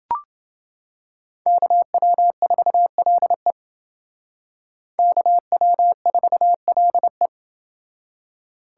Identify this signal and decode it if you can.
Morse code, heard as a keyed tone, about 29 wpm, 700 Hz. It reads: KW4LE KW4LE